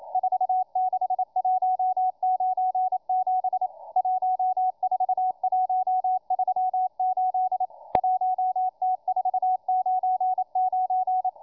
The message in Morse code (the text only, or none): 46197 14138 1T499